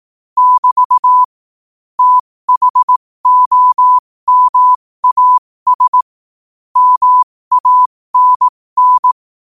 Morse code, heard as a keyed tone, about 18 wpm, spaced as for 15 wpm, 1000 Hz. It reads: <BT> THOMAS MANN
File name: qotdfiles / 15wpm